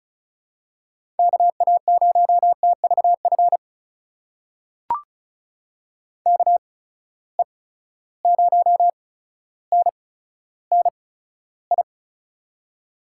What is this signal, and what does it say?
Morse code, heard as a keyed tone, about 34 wpm, spaced as for 15 wpm, 700 Hz. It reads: KA0TVF KE0NNI
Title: USA Call Signs - 6 Characters - 8x Character Spacing - 35wpm